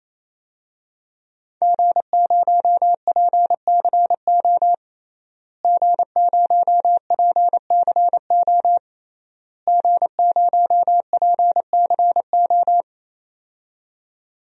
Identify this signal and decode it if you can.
Morse code, heard as a keyed tone, about 28 wpm, 700 Hz. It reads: G0PCO G0PCO G0PCO
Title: DXCC-100 Call Signs - 5 Characters - Repeated 3x - 28wpm